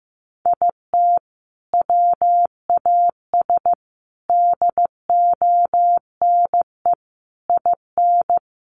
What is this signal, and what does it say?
Morse code, heard as a keyed tone, about 15 wpm, 700 Hz. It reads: IT WAS DONE IN
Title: Sentences from Top 1400 Words 15wpm